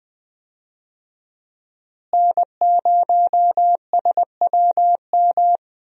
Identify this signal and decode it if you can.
Morse code, heard as a keyed tone, about 21 wpm, 700 Hz. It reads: N0SWM
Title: USA Call Signs - 5 Characters - Repeated 2x - 20wpm